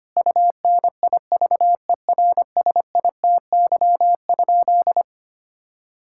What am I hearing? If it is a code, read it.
Morse code, heard as a keyed tone, about 25 wpm, 700 Hz. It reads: UNIVERSITY?
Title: Sentences from Top 1000 Words - Review Part 5 - 25wpm